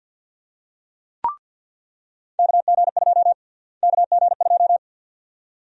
Morse code, heard as a keyed tone, about 48 wpm, 700 Hz. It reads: KG2 KG2
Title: USA Call Sign Prefixes - Repeated 2x - 50wpm